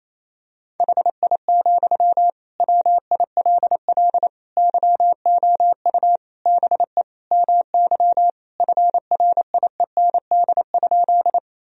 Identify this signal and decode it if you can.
Morse code, heard as a keyed tone, about 28 wpm, 700 Hz. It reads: HI, WILL YOU BE MY FRIEND?
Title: Sentences from Top 1000 Words - Review Part 3 - 28wpm